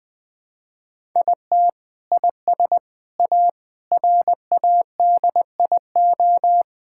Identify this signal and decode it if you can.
Morse code, heard as a keyed tone, about 20 wpm, 700 Hz. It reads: IT IS A RADIO
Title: Sentences from Top 1000 Words - Review Part 5 - 20wpm